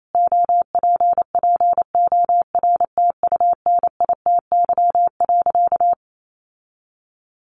Morse code, heard as a keyed tone, about 28 wpm, 700 Hz. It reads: OPPORTUNITY.
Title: Sentences from Top 1000 Words 28wpm